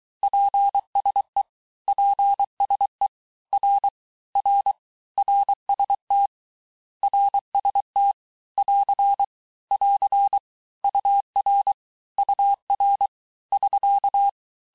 Morse code, heard as a keyed tone, about 23 wpm, 800 Hz. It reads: PSE PSE R R RST RST <AR> <AR> UR UR <SK>
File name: cours-du-lundi files / cours-1588-texte-abreviations-vitesse-17mpm-puis-22mpm